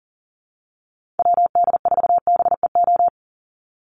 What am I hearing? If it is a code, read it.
Morse code, heard as a keyed tone, about 40 wpm, 700 Hz. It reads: WD4BEO
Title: USA Call Signs 40wpm